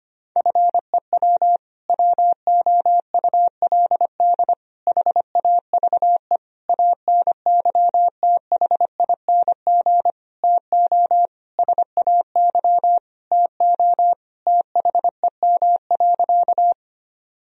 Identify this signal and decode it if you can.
Morse code, heard as a keyed tone, about 25 wpm, 700 Hz. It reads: FEW WOULD HAVE ANYTHING TO SAY TO THEM.
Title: Sentences from Top 1000 Words - Review Part 1 - 25wpm